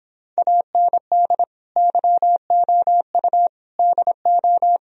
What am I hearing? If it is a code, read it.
Morse code, heard as a keyed tone, about 25 wpm, 700 Hz. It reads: AND YOU DO
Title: Speed-Racing - Sets of 4 Words 22wpm